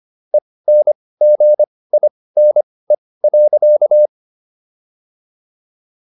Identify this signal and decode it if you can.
Morse code, heard as a keyed tone, about 25 wpm, spaced as for 18 wpm, 600 Hz. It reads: ENGINE.